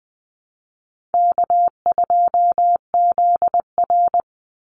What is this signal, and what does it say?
Morse code, heard as a keyed tone, about 20 wpm, 700 Hz. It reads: K2ZR
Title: Call Signs Repeated 2x - 20wpm